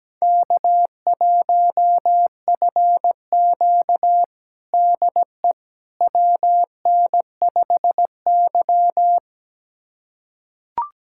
Morse code, heard as a keyed tone, about 17 wpm, 700 Hz. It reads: K1FQ DE WN5Y
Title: USA Callsign DE Callsign - 4 Characters - 17wpm